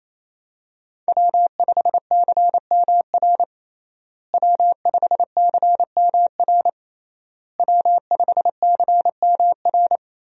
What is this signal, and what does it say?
Morse code, heard as a keyed tone, about 28 wpm, 700 Hz. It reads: W5CMR W5CMR W5CMR
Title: USA Call Signs - 5 Characters - Repeated 3x - 28wpm